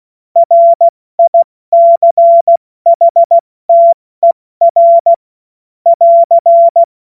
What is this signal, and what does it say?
Morse code, heard as a keyed tone, about 16 wpm, 680 Hz. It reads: RICHTER <AR>
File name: Morsekurs / Witze / Witz 25. Oktober 2022 70 F80